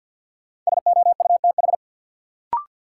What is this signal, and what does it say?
Morse code, heard as a keyed tone, about 53 wpm, 700 Hz. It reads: SOUTH
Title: Sentences from Top 400 Words - New Word Drill - 50wpm